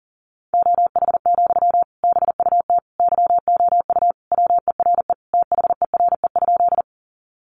Morse code, heard as a keyed tone, about 40 wpm, 700 Hz. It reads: OH, BUT YOU WERE THERE?
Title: Sentences from Top 500 Words 40wpm